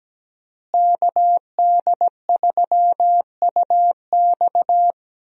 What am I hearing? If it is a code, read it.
Morse code, heard as a keyed tone, about 17 wpm, 700 Hz. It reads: KD3UX
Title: USA Call Signs - 5 Characters - Repeated 3x - 17wpm